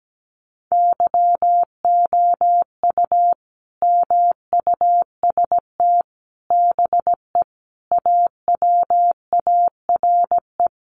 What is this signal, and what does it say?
Morse code, heard as a keyed tone, about 17 wpm, 700 Hz. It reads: YOU MUST BE AWARE
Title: Sentences from Top 1200 Words 17wpm